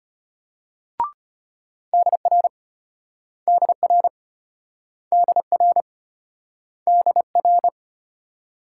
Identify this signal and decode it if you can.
Morse code, heard as a keyed tone, about 33 wpm, 700 Hz. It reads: DR DR DR DR